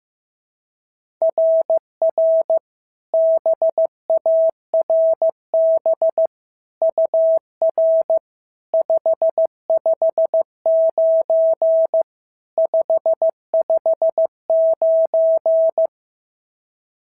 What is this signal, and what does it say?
Morse code, heard as a keyed tone, about 13 wpm, 650 Hz. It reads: RR BARB UR 559 559
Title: SOTA QSO Practice 15wpm